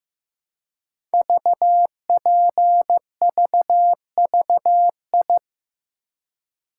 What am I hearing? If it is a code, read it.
Morse code, heard as a keyed tone, about 15 wpm, 700 Hz. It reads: VPVVI
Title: Code Groups - 5 Letters - 15wpm